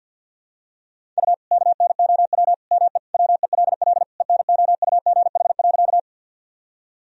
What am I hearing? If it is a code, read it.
Morse code, heard as a keyed tone, about 51 wpm, 700 Hz. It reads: U KNOW ME WELL ENOUGH.
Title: Sentences from Top 400 Words and Abbreviations 50wpm